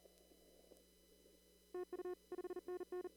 Morse code, heard as a keyed tone, about 40 wpm, 1440 Hz, with noise. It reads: TU 5NN
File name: CW / kt5j wpx cw-op-k5pi